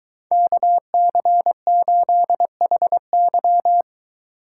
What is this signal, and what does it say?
Morse code, heard as a keyed tone, about 23 wpm, 700 Hz. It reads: KC8HY